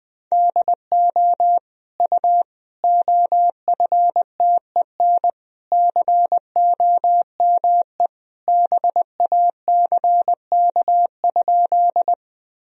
Morse code, heard as a keyed tone, about 20 wpm, 700 Hz. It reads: DO U OFTEN COME BACK?
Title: Sentences from Top 300 Words and Abbreviations 20wpm